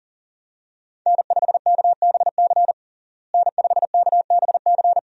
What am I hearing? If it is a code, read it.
Morse code, heard as a keyed tone, about 40 wpm, 700 Hz. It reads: N5KBC N5KBC
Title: USA Call Signs - 5 Characters - Repeated 3x - 40wpm